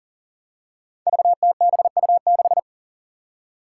Morse code, heard as a keyed tone, about 38 wpm, 700 Hz. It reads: VTBU6
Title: Five Challenging Letter-Number Combinations 40wpm